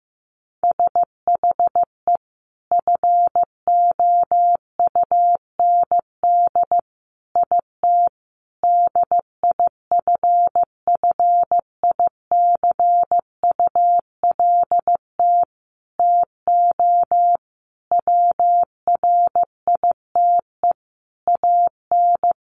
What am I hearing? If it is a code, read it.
Morse code, heard as a keyed tone, about 15 wpm, 700 Hz. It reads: SHE FOUND IT DIFFICULT TO WRITE AN
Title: Sentences from Top 1500 Words 15wpm